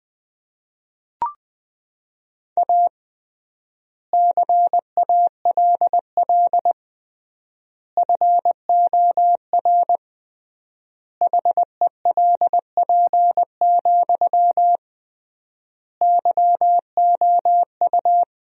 Morse code, heard as a keyed tone, about 20 wpm, 700 Hz. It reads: A CALL FOR HELP, YOU
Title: 3x Word-Spacing - Sentences from Top 300 Words - 20wpm